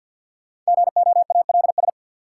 Morse code, heard as a keyed tone, about 51 wpm, 700 Hz. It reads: GOALS
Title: Sentences from Top 1000 Words - Review Part 6 - New Word Drill - 50wpm